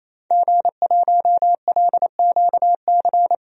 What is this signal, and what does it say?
Morse code, heard as a keyed tone, about 28 wpm, 700 Hz. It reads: G1LQC
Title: DXCC-100 Call Signs - 5 Characters - 28wpm